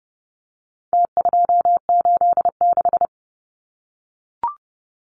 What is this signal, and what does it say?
Morse code, heard as a keyed tone, about 30 wpm, 700 Hz. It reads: T286
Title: Numbers - Contest Style - 4-Digit 30wpm